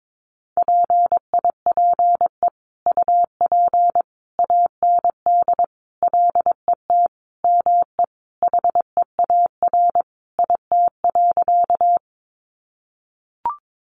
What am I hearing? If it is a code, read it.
Morse code, heard as a keyed tone, about 22 wpm, 700 Hz. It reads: PIPE UP AND LET ME HEAR IT.
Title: Sentences from Top 1500 Words 22wpm